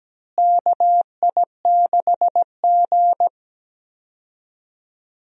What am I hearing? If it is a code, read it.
Morse code, heard as a keyed tone, about 17 wpm, 700 Hz. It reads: KI6G
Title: USA Call Signs - 4 Characters - Repeated 3x - 17wpm